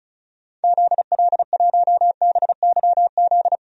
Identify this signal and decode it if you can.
Morse code, heard as a keyed tone, about 36 wpm, 700 Hz. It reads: ZL1BYZ
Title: DXCC-100 Call Signs - 6 Characters - Repeated 2x - 35wpm